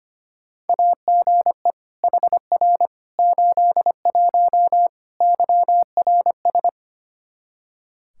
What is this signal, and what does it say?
Morse code, heard as a keyed tone, about 25 wpm, 700 Hz. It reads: AGE HR 81 YRS
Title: Practice QSO - Exchange 3 - More Info - 25wpm